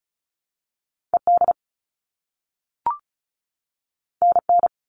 Morse code, heard as a keyed tone, about 36 wpm, 700 Hz. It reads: ED NN
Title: All Cut Numbers - 2-Digit 35wpm